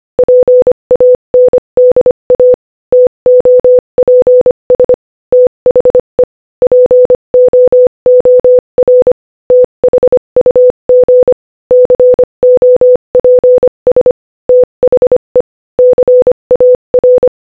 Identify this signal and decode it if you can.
Morse code, heard as a keyed tone, about 25 wpm, 500 Hz. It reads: PANDA TOPS THE POOR THUG COPS THE CAR
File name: Morse-Code-Audio-Speed-25-3